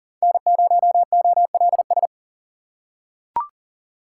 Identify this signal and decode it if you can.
Morse code, heard as a keyed tone, about 37 wpm, 700 Hz. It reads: N0OLS
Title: USA Call Signs - 4x Character Spacing - 40wpm